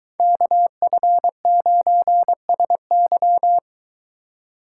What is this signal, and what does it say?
Morse code, heard as a keyed tone, about 22 wpm, 700 Hz. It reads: KF9SY